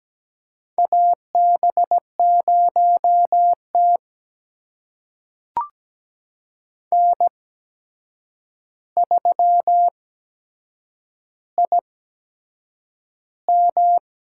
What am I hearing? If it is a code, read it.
Morse code, heard as a keyed tone, about 17 wpm, spaced as for 7 wpm, 700 Hz. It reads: AB0T N3IM
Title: USA Call Signs - 4 Characters - 8x Character Spacing - 17wpm